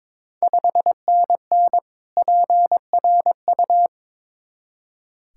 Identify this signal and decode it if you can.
Morse code, heard as a keyed tone, about 22 wpm, 700 Hz. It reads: 5NN PRU
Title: Ontario QSO Party Exchange 22wpm